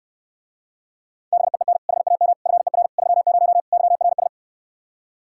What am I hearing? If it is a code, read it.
Morse code, heard as a keyed tone, about 69 wpm, 700 Hz. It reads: BEEN HAM FER 30 YRS